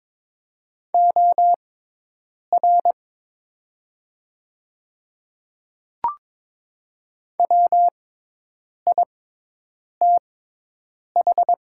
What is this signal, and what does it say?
Morse code, heard as a keyed tone, about 22 wpm, spaced as for 8 wpm, 690 Hz. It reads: OR WITH
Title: 6x Character Spacing - Top 100 Words - 22wpm